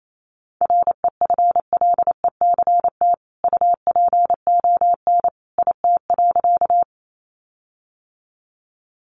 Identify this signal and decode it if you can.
Morse code, heard as a keyed tone, about 28 wpm, 700 Hz. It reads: REFLECT UPON IT.
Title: Sentences from Top 1500 Words 28wpm